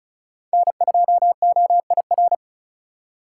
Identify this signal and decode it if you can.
Morse code, heard as a keyed tone, about 34 wpm, 700 Hz. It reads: N2OIR
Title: USA Call Signs - 5 Characters - 3x Character Spacing - 35wpm